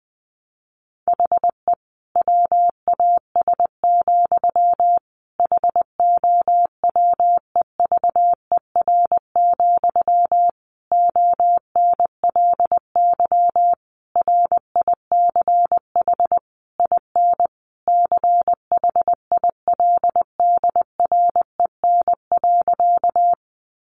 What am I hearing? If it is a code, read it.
Morse code, heard as a keyed tone, about 20 wpm, 700 Hz. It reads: HE WAS, HOWEVER, ONLY RICH IN CHILDREN.
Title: Sentences from Top 1100 Words 20wpm